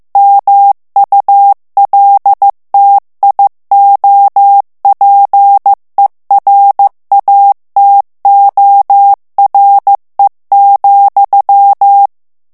Morse code, heard as a keyed tone, about 15 wpm, 800 Hz. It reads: MULTIOPERATORE,
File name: morse mp3 / ITALIEN texte clair 2 vitesse 14 mots minute